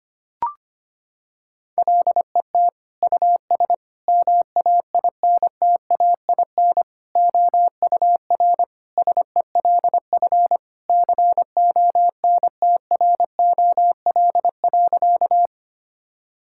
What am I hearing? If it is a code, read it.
Morse code, heard as a keyed tone, about 25 wpm, 700 Hz. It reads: LET US MAINTAIN OUR SELF CONTROL.
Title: Sentences from Top 1000 Words - Review Part 5 - 25wpm